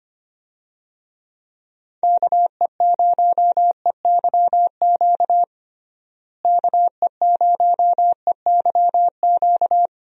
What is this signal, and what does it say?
Morse code, heard as a keyed tone, about 25 wpm, 700 Hz. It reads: KE0EYQ KE0EYQ
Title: USA Call Signs - 6 Characters - Repeated 2x - 25wpm